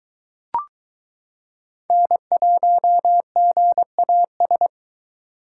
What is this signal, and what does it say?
Morse code, heard as a keyed tone, about 22 wpm, 700 Hz. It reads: N1GAS